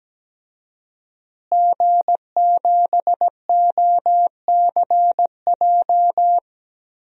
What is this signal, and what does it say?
Morse code, heard as a keyed tone, about 17 wpm, 700 Hz. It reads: G7OCJ
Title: DXCC-100 Call Signs - 5 Characters - Repeated 3x - 17wpm